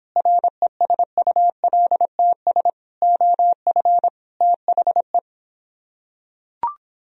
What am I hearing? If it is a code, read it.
Morse code, heard as a keyed tone, about 26 wpm, 700 Hz. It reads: RESULTS OF THE